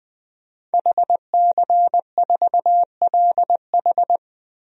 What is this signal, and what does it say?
Morse code, heard as a keyed tone, about 20 wpm, 700 Hz. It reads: HC4LH
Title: DXCC-100 Call Signs - 5 Characters - 20wpm